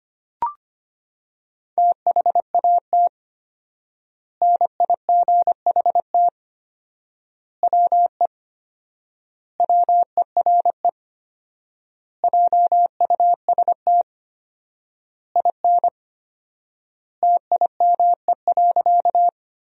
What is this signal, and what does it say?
Morse code, heard as a keyed tone, about 25 wpm, 700 Hz. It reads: THAT NIGHT WE WERE JUST IN TIME.
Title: 4x Word-Spacing - Sentences from Top 300 Words - 25wpm